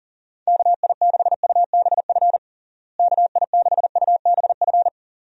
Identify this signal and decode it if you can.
Morse code, heard as a keyed tone, about 40 wpm, 700 Hz. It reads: KI6UBF KI6UBF
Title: USA Call Signs - 6 Characters - Repeated 2x - 40wpm